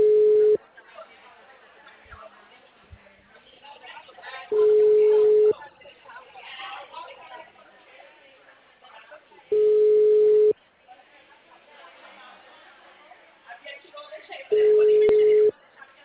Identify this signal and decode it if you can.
Ringback tone (Europe, 425 Hz), with speech.